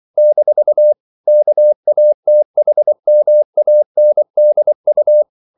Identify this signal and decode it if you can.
Morse code, heard as a keyed tone, about 24 wpm, 600 Hz. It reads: - KATHMANDU
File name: CW / Practice / NationalCapitals24